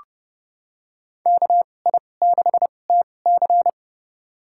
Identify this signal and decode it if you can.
Morse code, heard as a keyed tone, about 30 wpm, spaced as for 22 wpm, 700 Hz. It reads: KI6TC
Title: USA Call Signs - 2x Character Spacing - 30wpm